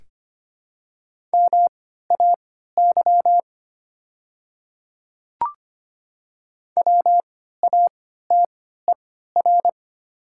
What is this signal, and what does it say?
Morse code, heard as a keyed tone, about 25 wpm, spaced as for 14 wpm, 700 Hz. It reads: MAY WATER
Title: 3x Character Spacing - Top 100 Words - 25wpm